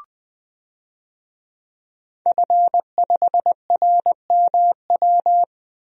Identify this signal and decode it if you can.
Morse code, heard as a keyed tone, about 21 wpm, 700 Hz. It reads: F5RMW
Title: DXCC-100 Call Signs - 5 Characters - Repeated 3x - 20wpm